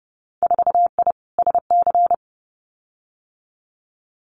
Morse code, heard as a keyed tone, about 30 wpm, 700 Hz. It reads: 4I SC
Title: Winter Field Day Exchanges 30wpm